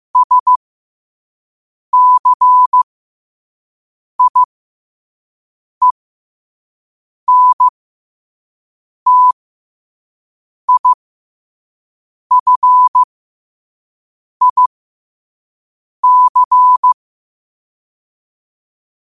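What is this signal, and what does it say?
Morse code, heard as a keyed tone, about 15 wpm, spaced as for 5 wpm, 1000 Hz. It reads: SCIENTIFIC
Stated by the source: Morse sent at 5 wpm.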